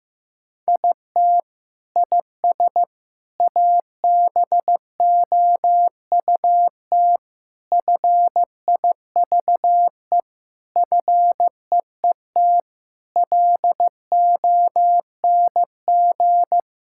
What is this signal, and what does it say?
Morse code, heard as a keyed tone, about 15 wpm, 700 Hz. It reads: IT IS ABOUT FIVE FEET LONG
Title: Sentences from Top 1000 Words - Review Part 6 - 15wpm